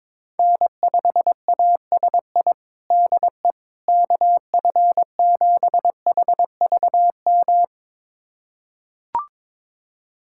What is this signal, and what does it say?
Morse code, heard as a keyed tone, about 21 wpm, 700 Hz. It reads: N5ASI DE KF7HVM
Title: USA Callsign DE Callsign 22wpm